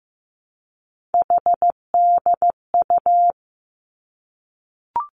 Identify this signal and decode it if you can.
Morse code, heard as a keyed tone, about 15 wpm, 700 Hz. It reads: HDU
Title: Three Challenging Letter-Number Combinations 15wpm